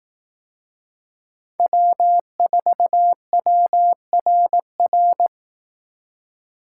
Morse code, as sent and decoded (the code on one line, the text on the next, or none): .-- ....- .-- .-. .-.
W4WRR